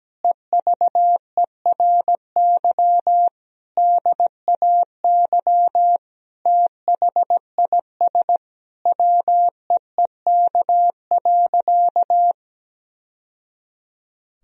Morse code, heard as a keyed tone, about 17 wpm, 700 Hz. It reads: EVERY DAY THIS WEEK.
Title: Sentences from Top 1000 Words - Review Part 2 - 17wpm